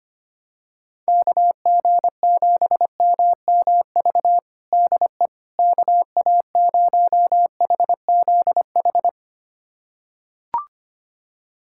Morse code, heard as a keyed tone, about 25 wpm, 700 Hz. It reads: KG7MMV DE KA0HZH
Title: USA Callsign DE Callsign 25wpm